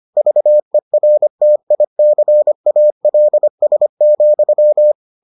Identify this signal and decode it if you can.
Morse code, heard as a keyed tone, about 25 wpm, 600 Hz. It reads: VERTICALS,